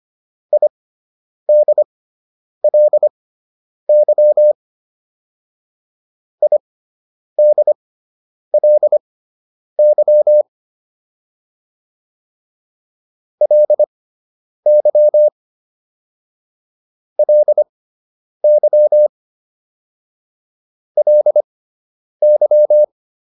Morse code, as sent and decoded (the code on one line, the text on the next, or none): .. -.. .-.. -.-- / .. -.. .-.. -.-- / .-.. -.-- / .-.. -.-- / .-.. -.--
IDLY IDLY LY LY LY